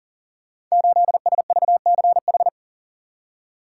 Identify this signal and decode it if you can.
Morse code, heard as a keyed tone, about 42 wpm, 690 Hz. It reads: 8SVCH
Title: Code Groups - 5 Letter:Numbers - 40wpm